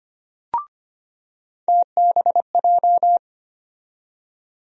Morse code, heard as a keyed tone, about 25 wpm, 700 Hz. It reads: TBJ
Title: USA Call Sign Postfixes - 3 Letters 25wpm